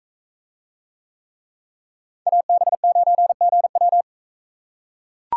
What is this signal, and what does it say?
Morse code, heard as a keyed tone, about 42 wpm, 700 Hz. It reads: AB9GW